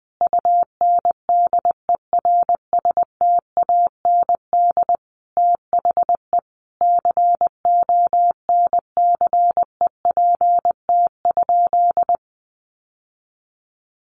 Morse code, heard as a keyed tone, about 20 wpm, 700 Hz. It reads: UNDERSTAND THE CONCEPT?
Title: Sentences from Top 1100 Words 20wpm